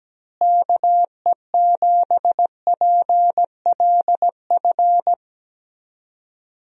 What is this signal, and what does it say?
Morse code, heard as a keyed tone, about 17 wpm, 700 Hz. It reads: KE7PLF
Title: USA Call Signs - 6 Characters - Repeated 3x - 17wpm